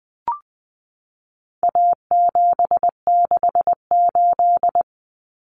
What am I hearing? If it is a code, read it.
Morse code, heard as a keyed tone, about 20 wpm, 700 Hz. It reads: A768
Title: Common Cut Numbers - 4-Digit 20wpm